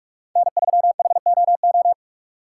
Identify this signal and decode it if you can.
Morse code, heard as a keyed tone, about 46 wpm, 700 Hz. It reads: N3HOO
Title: USA Call Signs - 5 Characters - 2x Character Spacing - 45wpm